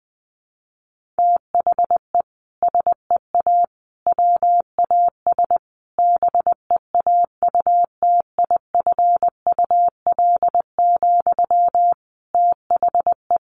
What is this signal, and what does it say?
Morse code, heard as a keyed tone, about 20 wpm, 700 Hz. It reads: THE SEA WAS BEAUTIFUL, THE
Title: Sentences from Top 1200 Words 20wpm